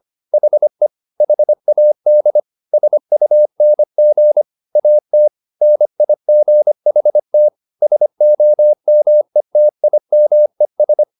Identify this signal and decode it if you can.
Morse code, heard as a keyed tone, about 25 wpm, 600 Hz. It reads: HE HAD SUNG AT NIGHT SOMETIMES
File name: the old man and the sea part 1 25wpm